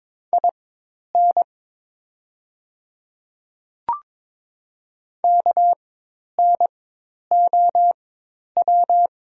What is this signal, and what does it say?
Morse code, heard as a keyed tone, about 23 wpm, spaced as for 10 wpm, 700 Hz. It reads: IN KNOW
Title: 4x Character Spacing - Top 100 Words - 22wpm